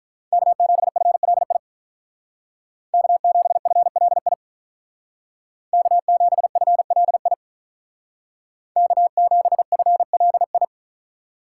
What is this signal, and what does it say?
Morse code, heard as a keyed tone, about 43 wpm, 700 Hz. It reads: K7FLI K7FLI K7FLI K7FLI
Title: Speed Racing - USA 1x3 Call Signs 35wpm